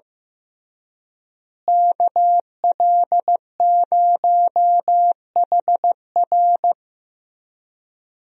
Morse code, heard as a keyed tone, about 15 wpm, 700 Hz. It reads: KL0HR
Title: USA Call Signs - 5 Characters - Repeated 3x - 15wpm